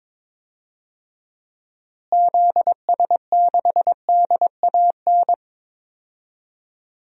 Morse code, heard as a keyed tone, about 22 wpm, 700 Hz. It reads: ZS6DAN
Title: DXCC-100 Call Signs - 6 Characters - Repeated 3x - 22wpm